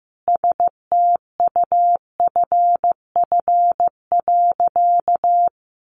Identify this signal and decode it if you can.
Morse code, heard as a keyed tone, about 15 wpm, 700 Hz. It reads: STUFF.
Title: Sentences from Top 1400 Words 15wpm